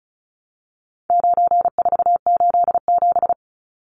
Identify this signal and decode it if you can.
Morse code, heard as a keyed tone, about 34 wpm, 700 Hz. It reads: 9487
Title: Numbers - 4-Digit 35wpm